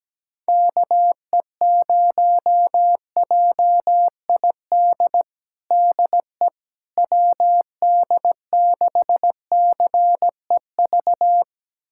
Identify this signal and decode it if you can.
Morse code, heard as a keyed tone, about 17 wpm, 700 Hz. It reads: KE0JID DE WD6CEV
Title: USA Callsign DE Callsign - 6 Characters - 17wpm